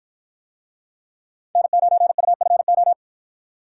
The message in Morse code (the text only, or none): N9UFO